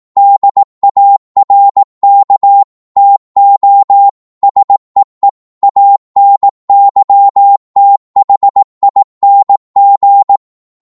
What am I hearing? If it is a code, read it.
Morse code, heard as a keyed tone, about 18 wpm, 800 Hz. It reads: DARK TO SEE ANYTHING
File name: Morse / psg-18wpm